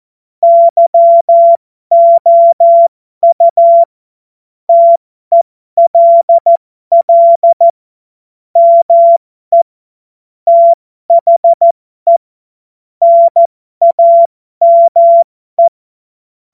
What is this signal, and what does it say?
Morse code, heard as a keyed tone, about 14 wpm, spaced as for 12 wpm, 680 Hz. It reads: YOU TELL ME THE NAME
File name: Morsekurs / Witze